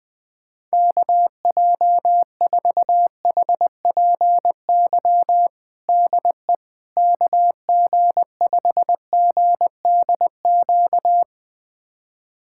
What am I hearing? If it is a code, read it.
Morse code, heard as a keyed tone, about 20 wpm, 700 Hz. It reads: KJ4HPY DE KG5GDQ
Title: USA Callsign DE Callsign 20wpm